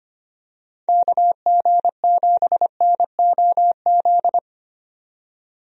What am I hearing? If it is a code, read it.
Morse code, heard as a keyed tone, about 25 wpm, 700 Hz. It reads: KG7NOZ
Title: USA Call Signs - 6 Characters - 25wpm